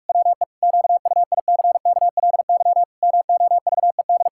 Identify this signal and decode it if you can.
Morse code, heard as a keyed tone, about 41 wpm, 700 Hz. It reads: WE QUICKLY MOVED
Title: Pangram Challenge v2 - 45wpm